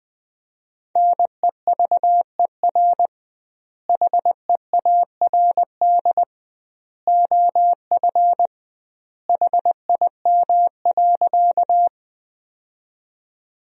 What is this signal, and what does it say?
Morse code, heard as a keyed tone, about 21 wpm, 700 Hz. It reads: NEVER HEARD OF HIM.
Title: 2x Word-Spacing - Sentences from Top 400 Words - 20wpm